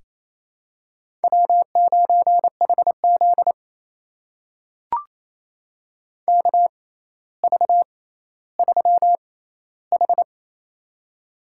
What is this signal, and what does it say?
Morse code, heard as a keyed tone, about 28 wpm, spaced as for 14 wpm, 700 Hz. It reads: W9HZ KV3H
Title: USA Call Signs - 4 Characters - 6x Character Spacing - 28wpm